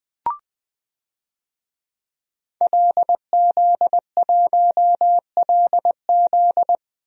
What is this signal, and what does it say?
Morse code, heard as a keyed tone, about 20 wpm, 700 Hz. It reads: LZ1LZ
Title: DXCC-100 Call Signs - 5 Characters - 20wpm